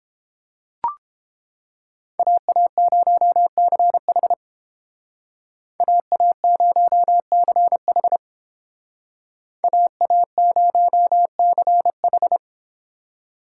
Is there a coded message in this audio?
Yes: Morse code from 2.2 to 12.4 s, heard as a keyed tone, about 31 wpm, 700 Hz. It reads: AA0CH AA0CH AA0CH